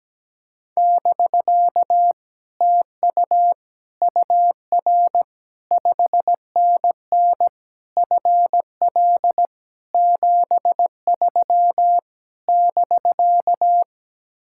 Morse code, heard as a keyed tone, about 17 wpm, 700 Hz. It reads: <BK> TU UR 5NN FL 73 <BK>
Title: Speed-Racing Common QSO Phrases - 17wpm